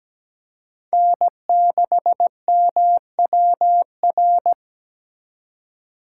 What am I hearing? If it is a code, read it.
Morse code, heard as a keyed tone, about 17 wpm, 700 Hz. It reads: N6MWR
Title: USA Call Signs - 5 Characters - 3x Character Spacing - 17wpm